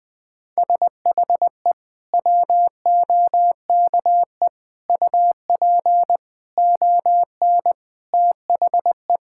Morse code, heard as a keyed tone, about 20 wpm, 700 Hz. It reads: SHE WOKE UP ON THE
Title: Sentences with 6-Word Phrases - 20wpm